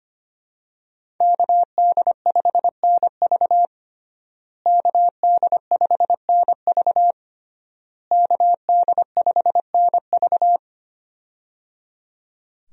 Morse code, heard as a keyed tone, about 25 wpm, 700 Hz. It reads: KD5NV KD5NV KD5NV
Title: USA Call Signs - 5 Characters - Repeated 3x - 25wpm